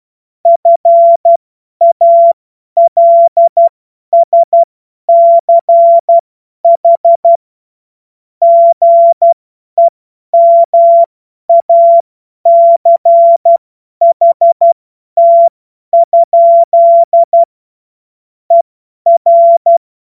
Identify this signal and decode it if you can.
Morse code, heard as a keyed tone, about 12 wpm, spaced as for 10 wpm, 680 Hz. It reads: FALSCH GEMACHT? ER
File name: Morsekurs / Witze / Witz 07. Juni 2022 50 F70